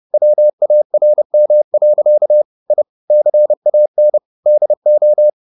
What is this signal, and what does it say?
Morse code, heard as a keyed tone, about 30 wpm, 600 Hz. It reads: WARM. I CAN DO
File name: the old man and the sea / the old man and the sea 1 30wpm